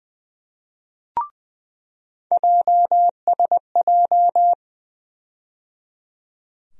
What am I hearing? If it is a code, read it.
Morse code, heard as a keyed tone, about 20 wpm, 700 Hz. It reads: JSJ
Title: USA Call Sign Postfixes - 3 Letters 20wpm